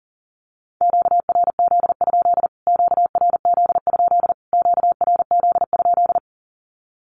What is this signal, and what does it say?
Morse code, heard as a keyed tone, about 40 wpm, 700 Hz. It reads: QRZ? QRZ? QRZ?
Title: Q-Codes Repeated 3x 40wpm